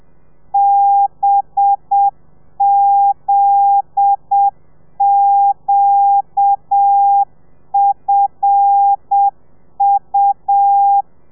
Morse code, heard as a keyed tone, about 7 wpm, 800 Hz. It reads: BZQFU